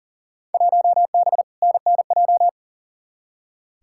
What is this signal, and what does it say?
Morse code, heard as a keyed tone, about 40 wpm, 700 Hz. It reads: 1B NNJ
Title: Field Day Exchanges Encore - 40wpm